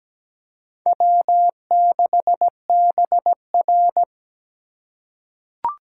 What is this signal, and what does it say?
Morse code, heard as a keyed tone, about 17 wpm, 700 Hz. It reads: W6BR
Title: USA Call Signs - 2x Character Spacing - 17wpm